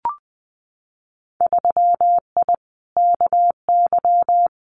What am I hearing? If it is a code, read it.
Morse code, heard as a keyed tone, about 20 wpm, 700 Hz. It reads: 3I KY
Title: Winter Field Day Exchanges 20wpm